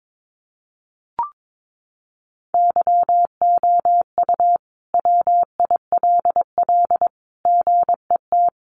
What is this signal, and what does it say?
Morse code, heard as a keyed tone, about 22 wpm, 700 Hz. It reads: YOU WILL GET
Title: Sentences from Top 200 Words 22wpm